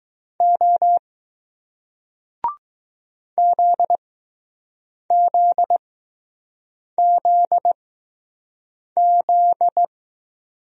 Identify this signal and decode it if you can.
Morse code, heard as a keyed tone, about 20 wpm, spaced as for 6 wpm, 700 Hz. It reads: O ZZZZ